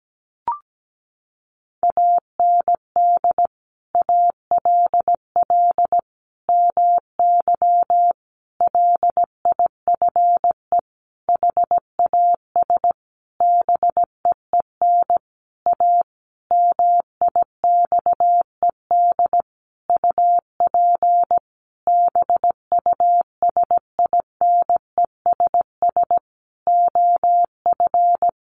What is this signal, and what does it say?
Morse code, heard as a keyed tone, about 17 wpm, 700 Hz. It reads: AND ALL MY LIFE HAS BEEN A MIXED UP BUSINESS OF
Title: Sentences from Top 1500 Words 17wpm